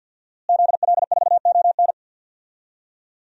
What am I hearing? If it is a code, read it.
Morse code, heard as a keyed tone, about 47 wpm, 700 Hz. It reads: ZL4ON
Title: DXCC-100 Call Signs - 5 Characters - Repeated 3x - 50wpm